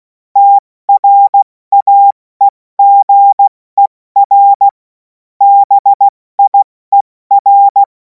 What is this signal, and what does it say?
Morse code, heard as a keyed tone, about 16 wpm, 800 Hz. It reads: TRAEGER BIER
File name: Morsekurs / Witze / Witz 07. April 2026 70 F80